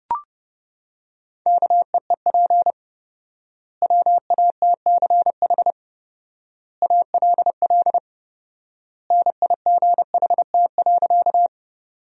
Morse code, heard as a keyed tone, about 32 wpm, 700 Hz. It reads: KEEP WATCH ALL NIGHT.
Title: 4x Word-Spacing - Sentences from Top 300 Words - 30wpm